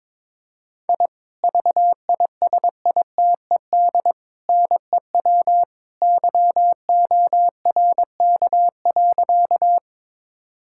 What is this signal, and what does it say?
Morse code, heard as a keyed tone, about 22 wpm, 700 Hz. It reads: I VISITED NEW YORK.
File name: practice / 22wpm / Sentences with 2-Word Phrases - 22wpm